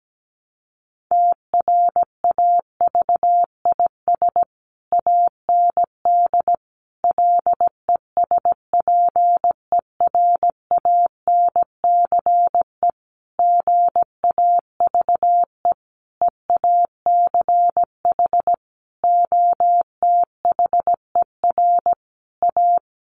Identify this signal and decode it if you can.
Morse code, heard as a keyed tone, about 17 wpm, 700 Hz. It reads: TRAVIS AND LESPERANCE GAVE EACH OTHER A
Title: A Sound Of Thunder 17wpm